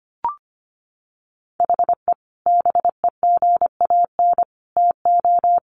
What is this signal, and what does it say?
Morse code, heard as a keyed tone, about 25 wpm, 700 Hz. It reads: HE BEGAN TO
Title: Sentences from Top 500 Words 25wpm